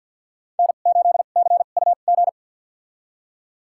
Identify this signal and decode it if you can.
Morse code, heard as a keyed tone, about 46 wpm, spaced as for 34 wpm, 700 Hz. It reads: N8CUG
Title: USA Call Signs - 2x Character Spacing - 50wpm